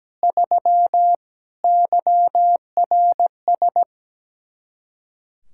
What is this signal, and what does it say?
Morse code, heard as a keyed tone, about 18 wpm, 700 Hz. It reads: 3 YRS
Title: Speed-Racing Common QSO Phrases - 17wpm